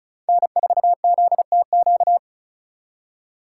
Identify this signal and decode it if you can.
Morse code, heard as a keyed tone, about 45 wpm, 700 Hz. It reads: N4ZTQ